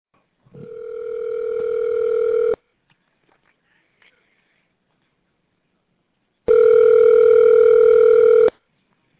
Ringback tone (North America), 440 + 480 Hz.